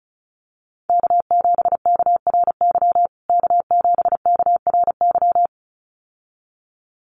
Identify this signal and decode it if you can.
Morse code, heard as a keyed tone, about 34 wpm, 700 Hz. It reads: K7KRY K7KRY
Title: USA Call Signs Repeated 2x 35wpm